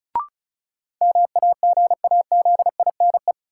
Morse code, heard as a keyed tone, about 36 wpm, 700 Hz. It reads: MAGAZINE
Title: Sentences from Top 1000 Words - Review Part 7 - New Word Drill - 35wpm